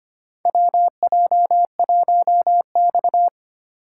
Morse code, heard as a keyed tone, about 25 wpm, 700 Hz. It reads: WJ1X
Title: USA Call Signs - 4 Characters - 2x Character Spacing - 25wpm